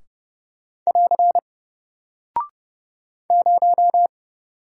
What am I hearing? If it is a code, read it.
Morse code, heard as a keyed tone, about 30 wpm, 700 Hz. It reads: <AR> 0
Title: Single Character (Letter:Number:Punctuation:Prosign) Alt 30wpm